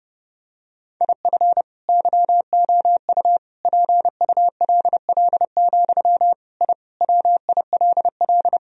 Morse code, heard as a keyed tone, about 30 wpm, 700 Hz. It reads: IF YOU PULL, I WILL
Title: Sentences from Top 1000 Words - Review Part 2 - 30wpm